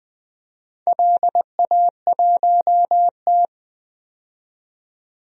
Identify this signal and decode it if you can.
Morse code, heard as a keyed tone, about 20 wpm, 700 Hz. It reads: LA1T
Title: DXCC-100 Call Signs - 4 Characters - 20wpm